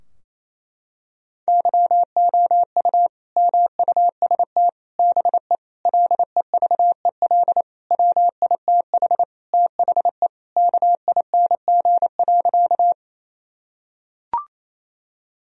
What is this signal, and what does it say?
Morse code, heard as a keyed tone, about 28 wpm, 700 Hz. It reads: YOU MUST BE LEVEL WITH THE KING.
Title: Sentences from Top 1000 Words - Review Part 4 - 28wpm